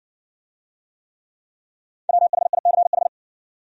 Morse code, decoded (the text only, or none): WHICH